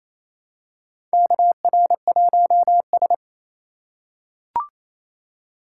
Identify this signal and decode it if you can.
Morse code, heard as a keyed tone, about 28 wpm, 700 Hz. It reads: KR1S
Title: USA Call Signs - 2x Character Spacing - 28wpm